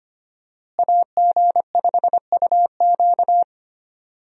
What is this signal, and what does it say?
Morse code, heard as a keyed tone, about 25 wpm, 700 Hz. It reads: AG5UQ
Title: USA Call Signs - 5 Characters - Repeated 3x - 25wpm